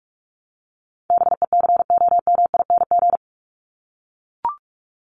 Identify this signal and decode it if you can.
Morse code, heard as a keyed tone, about 44 wpm, 700 Hz. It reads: BECOMING
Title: ING Suffix - Top 100 Words with Base Words 45wpm